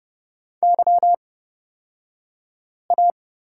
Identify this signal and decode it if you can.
Morse code, heard as a keyed tone, about 30 wpm, spaced as for 5 wpm, 700 Hz. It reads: YA
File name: Lesson 37d - Letter x - Two Characters 30wpm